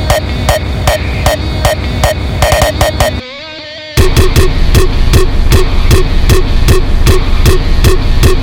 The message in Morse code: none

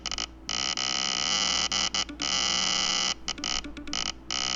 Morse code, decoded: none

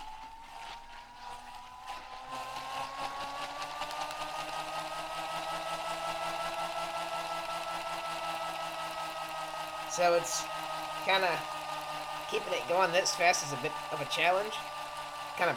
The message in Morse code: none

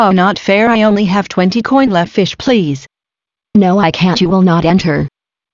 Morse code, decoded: none

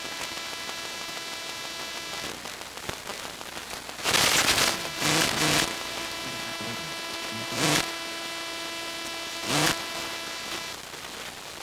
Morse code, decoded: none

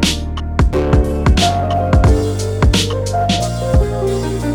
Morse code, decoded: none

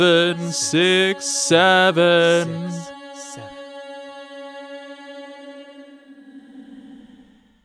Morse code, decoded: none

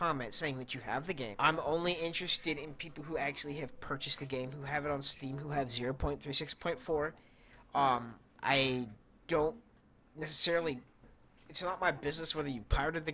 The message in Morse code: none

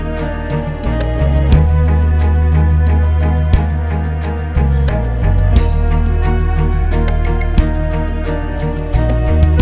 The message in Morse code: none

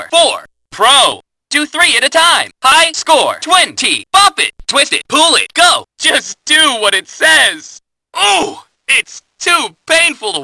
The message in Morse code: none

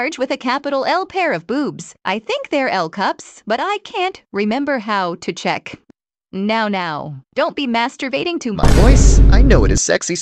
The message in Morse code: none

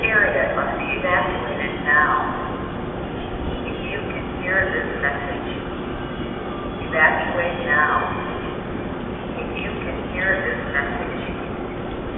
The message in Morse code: none

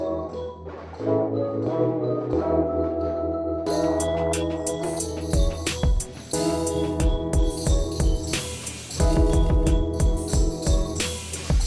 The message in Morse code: none

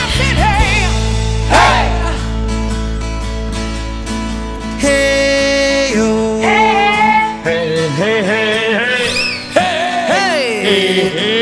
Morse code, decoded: none

